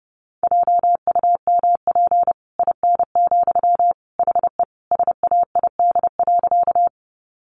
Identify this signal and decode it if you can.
Morse code, heard as a keyed tone, about 30 wpm, 700 Hz. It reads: JUMP IN, HE SAID.
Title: Sentences from Top 1300 Words 30wpm